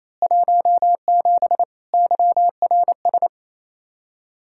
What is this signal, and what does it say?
Morse code, heard as a keyed tone, about 28 wpm, 700 Hz. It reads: 17 YRS
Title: Practice QSO - Exchange 3 - More Info - 28wpm